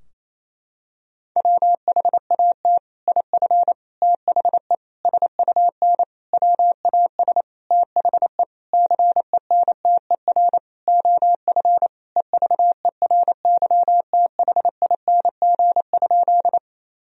Morse code, heard as a keyed tone, about 27 wpm, 700 Hz. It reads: WHAT IF THE SUN WAS THE CENTER OF EVERYTHING?
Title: Sentences from Top 1000 Words - Review Part 2 - 28wpm